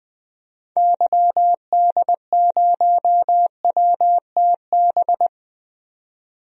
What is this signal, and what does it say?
Morse code, heard as a keyed tone, about 20 wpm, 700 Hz. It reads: YD0WTB
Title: DXCC-100 Call Signs - 6 Characters - 20wpm